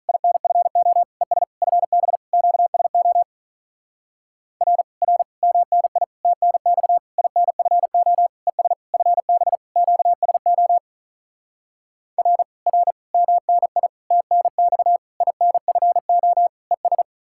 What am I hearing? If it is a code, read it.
Morse code, heard as a keyed tone, about 48 wpm, 700 Hz. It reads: INFO ES FB QSO R R MNI TNX INFO ES FB QSO R R MNI TNX INFO ES